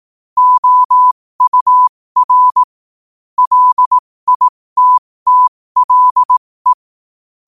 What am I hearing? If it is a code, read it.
Morse code, heard as a keyed tone, about 18 wpm, spaced as for 15 wpm, 1000 Hz. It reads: OUR LITTLE
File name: qotdfiles / 15wpm